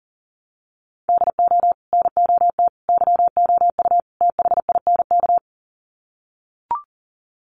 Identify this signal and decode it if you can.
Morse code, heard as a keyed tone, about 40 wpm, 700 Hz. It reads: DO NOT YOU THINK
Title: Sets of 4 Words 40wpm